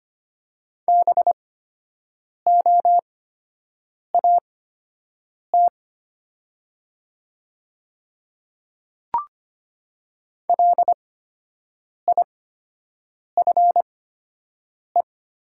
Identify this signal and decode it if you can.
Morse code, heard as a keyed tone, about 25 wpm, spaced as for 7 wpm, 700 Hz. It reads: BOAT LIFE
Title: 8x Character Spacing - Top 500 Words - 25wpm